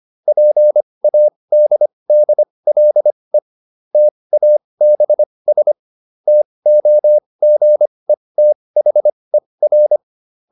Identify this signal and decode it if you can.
Morse code, heard as a keyed tone, about 25 wpm, spaced as for 20 wpm, 600 Hz. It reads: PADDLE TABS TOGETHER
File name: SS315 20